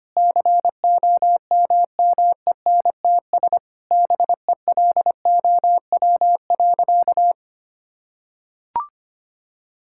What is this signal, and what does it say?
Morse code, heard as a keyed tone, about 25 wpm, 700 Hz. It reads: COMMENTS BELOW.
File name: practice / 25wpm / Sentences with 9-Word Phrases - 25wpm